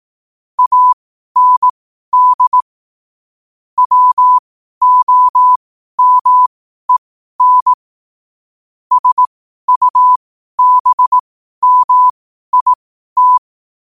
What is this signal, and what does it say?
Morse code, heard as a keyed tone, about 18 wpm, spaced as for 12 wpm, 1000 Hz. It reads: AND WOMEN SUBMIT
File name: qotdfiles / 12wpm